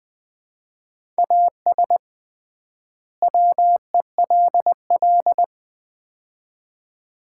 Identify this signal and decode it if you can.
Morse code, heard as a keyed tone, about 20 wpm, 700 Hz. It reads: AS WELL
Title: 3x Word-Spacing - Sets of 2 Words - 20wpm